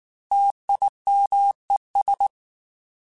Morse code, heard as a keyed tone, about 19 wpm, 800 Hz. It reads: TIMES
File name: Technical Subjects / Technical Subjects-mp3s / XC @18